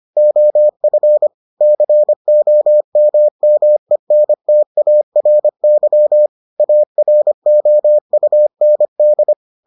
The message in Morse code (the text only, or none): OF COMMENTARY AROUND